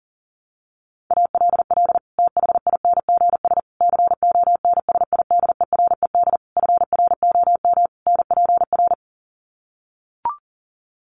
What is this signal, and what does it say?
Morse code, heard as a keyed tone, about 40 wpm, 700 Hz. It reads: ALL THINGS CONSIDERED FROM NPR
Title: Sets of 5 Words Encore 40wpm